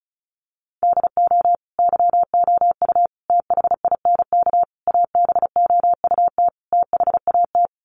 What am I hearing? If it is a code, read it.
Morse code, heard as a keyed tone, about 34 wpm, 700 Hz. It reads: DO YOU THINK ABOUT THAT
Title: Sets of 5 Words Encore2 35wpm